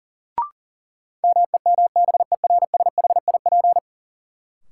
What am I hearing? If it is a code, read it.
Morse code, heard as a keyed tone, about 37 wpm, 700 Hz. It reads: MEMBERSHIP
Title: Sentences from Top 1400 Words - New Word Drill - 40wpm